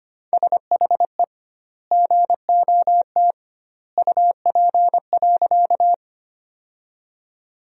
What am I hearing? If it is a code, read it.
Morse code, heard as a keyed tone, about 25 wpm, 700 Hz. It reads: SHE GOT UP.
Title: 2x Word-Spacing - Sentences from Top 300 Words - 25wpm